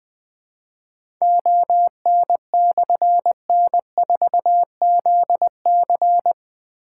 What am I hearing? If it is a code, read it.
Morse code, heard as a keyed tone, about 20 wpm, 700 Hz. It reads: ON/N4ZC
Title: Difficult Call Signs 20wpm